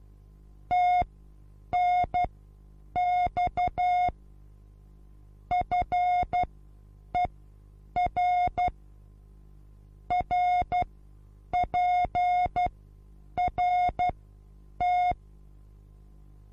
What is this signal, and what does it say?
Morse code, heard as a keyed tone, about 12 wpm, spaced as for 8 wpm, 720 Hz. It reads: TNX FER RPRT
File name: Onlinelehrgang / morsen / DJ4UF Morsen23